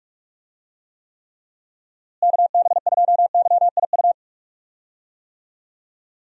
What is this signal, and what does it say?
Morse code, heard as a keyed tone, about 46 wpm, 700 Hz. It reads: KB2YIU
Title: USA Call Signs - 6 Characters - Repeated 3x - 45wpm